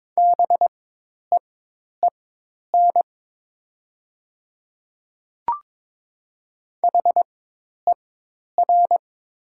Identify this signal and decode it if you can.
Morse code, heard as a keyed tone, about 22 wpm, spaced as for 10 wpm, 700 Hz. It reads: BEEN HER
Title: 4x Character Spacing - Top 100 Words - 22wpm